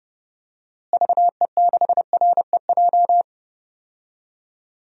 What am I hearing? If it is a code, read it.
Morse code, heard as a keyed tone, about 32 wpm, 700 Hz. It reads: VE6REJ
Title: DXCC-100 Call Signs - 6 Characters - Repeated 3x - 30wpm